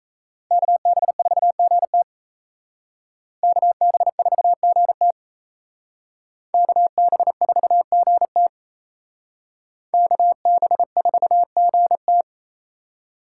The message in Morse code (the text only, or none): KB4GT KB4GT KB4GT KB4GT